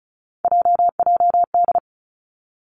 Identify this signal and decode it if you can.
Morse code, heard as a keyed tone, about 36 wpm, 700 Hz. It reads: JJD
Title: Three Challenging Letter-Number Combinations 35wpm